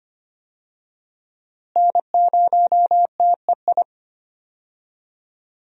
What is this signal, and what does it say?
Morse code, heard as a keyed tone, about 25 wpm, 700 Hz. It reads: N0TEI
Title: USA Call Signs - 5 Characters - Repeated 3x - 25wpm